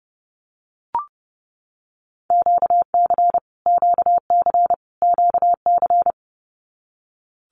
Morse code, heard as a keyed tone, about 30 wpm, 700 Hz. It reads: QC QC QC
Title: Canadian Provinces and Territories - Repeated 3x - 30wpm